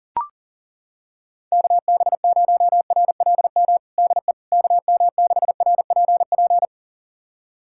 Morse code, heard as a keyed tone, about 40 wpm, 700 Hz. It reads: KB0RLM DE KM6RPP
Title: USA Callsign DE Callsign - 6 Characters - 40wpm